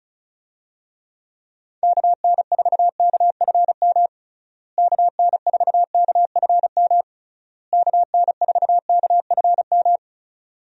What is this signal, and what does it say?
Morse code, heard as a keyed tone, about 34 wpm, 700 Hz. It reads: KN4KFM KN4KFM KN4KFM
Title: USA Call Signs - 6 Characters - Repeated 3x - 35wpm